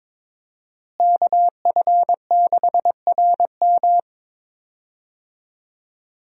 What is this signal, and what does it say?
Morse code, heard as a keyed tone, about 21 wpm, 700 Hz. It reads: KF6RM
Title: USA Call Signs - 5 Characters - Repeated 3x - 22wpm